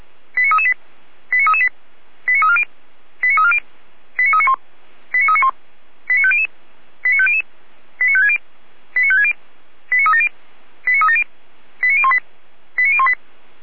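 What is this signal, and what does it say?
Five-tone selective call (ZVEI), heard as quick runs of five tones, tones of about 70 ms. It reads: 89338 89338 89340 89340 89391 89391 89500 89500 89560 89560 89370 89370 89018 89018
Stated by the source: ZVEI five-tone selective call received on an SDR, archived by Feuerwehr Orlamünde fire brigade.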